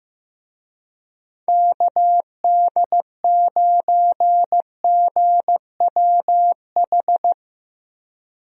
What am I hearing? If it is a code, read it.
Morse code, heard as a keyed tone, about 15 wpm, 700 Hz. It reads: KD9GWH
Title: USA Call Signs - 6 Characters - Repeated 2x - 15wpm